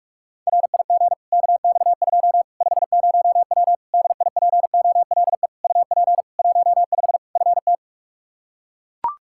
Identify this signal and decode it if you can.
Morse code, heard as a keyed tone, about 43 wpm, 700 Hz. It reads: RIG KX2 50W DIPOLE UP 15 FT
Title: K1USN SST QSO Practice 45wpm